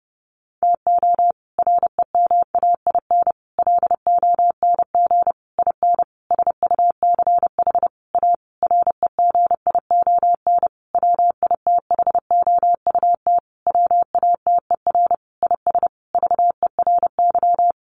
Morse code, heard as a keyed tone, about 30 wpm, 700 Hz. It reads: TO REMAIN LONG IN SUCH A REGION WITHOUT WATER IS VERY
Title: Sentences from Top 1500 Words 30wpm